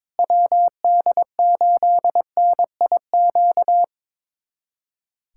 Morse code, heard as a keyed tone, about 22 wpm, 700 Hz. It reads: WD8NIQ
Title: USA Callsign DE Callsign 22wpm